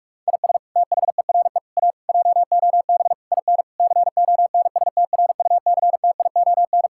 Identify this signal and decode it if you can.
Morse code, heard as a keyed tone, about 44 wpm, 700 Hz. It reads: IS THERE A JOB IN CONSTRUCTION
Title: Sentences from Top 1600 Words 45wpm